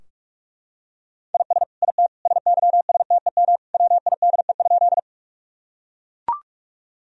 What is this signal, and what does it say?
Morse code, heard as a keyed tone, about 42 wpm, 700 Hz. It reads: IS IT SYSTEM WIDE?
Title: Sentences from Top 1000 Words - Review Part 3 - 45wpm